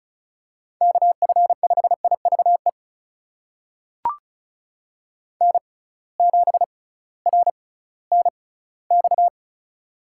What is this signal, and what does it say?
Morse code, heard as a keyed tone, about 36 wpm, spaced as for 19 wpm, 700 Hz. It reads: KF5IVE N7RNX
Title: USA Call Signs - 6x Character Spacing - 35wpm